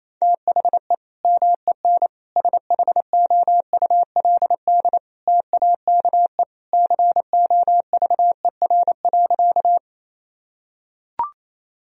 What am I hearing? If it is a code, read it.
Morse code, heard as a keyed tone, about 28 wpm, 700 Hz. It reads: THE MEN SHOULD TAKE COVER.
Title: Sentences from Top 1000 Words - Review Part 1 - 28wpm